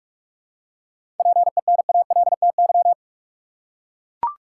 Morse code, heard as a keyed tone, about 44 wpm, 700 Hz. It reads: PENALTY
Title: Seven Letter Words - 45wpm